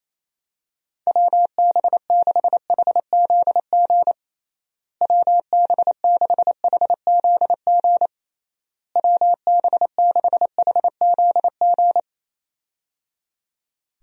Morse code, heard as a keyed tone, about 28 wpm, 700 Hz. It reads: WB6HZG WB6HZG WB6HZG
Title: USA Call Signs - 6 Characters - Repeated 3x - 28wpm